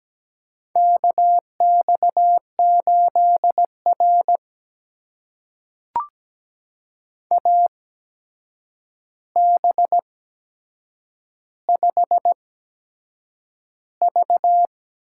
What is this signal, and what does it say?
Morse code, heard as a keyed tone, about 17 wpm, spaced as for 7 wpm, 700 Hz. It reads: KX8R AB5V
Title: USA Call Signs - 4 Characters - 8x Character Spacing - 17wpm